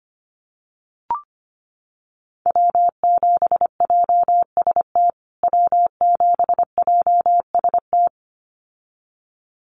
Morse code, heard as a keyed tone, about 25 wpm, 700 Hz. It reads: W7JST W7JST
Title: USA Call Signs Repeated 2x 25wpm